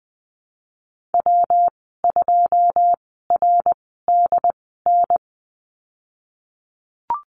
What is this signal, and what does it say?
Morse code, heard as a keyed tone, about 20 wpm, spaced as for 15 wpm, 700 Hz. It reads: W2RDN
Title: 167 Lesson 27g - Number 7 - Call Signs 20wpm